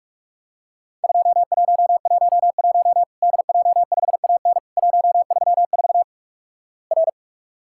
Morse code, heard as a keyed tone, about 46 wpm, 700 Hz. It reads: 2111 DJ5AN 134 R
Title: Worked All Europe DX Contest - QTC Exchange - 45wpm